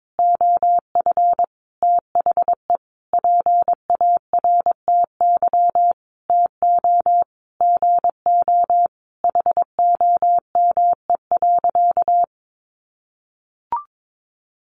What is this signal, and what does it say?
Morse code, heard as a keyed tone, about 22 wpm, 700 Hz. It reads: OF THE PARTY TO GO HOME.
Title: Sentences from Top 1400 Words 22wpm